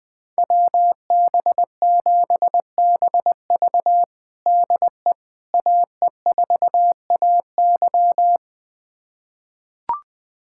Morse code, heard as a keyed tone, about 20 wpm, 700 Hz. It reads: WB7BV DE AE4AY
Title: USA Callsign DE Callsign - 5 Characters - 20wpm